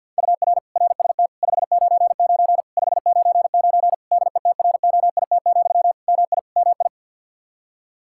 Morse code, heard as a keyed tone, about 49 wpm, 700 Hz. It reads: UR RST 599 599 DETROIT, MI MI
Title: Practice QSO - Exchange 1 - Beginning - 50wpm